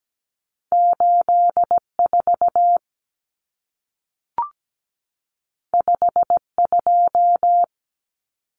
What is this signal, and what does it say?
Morse code, heard as a keyed tone, about 17 wpm, 700 Hz. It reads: 84 52
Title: Numbers - 2-Digit 17wpm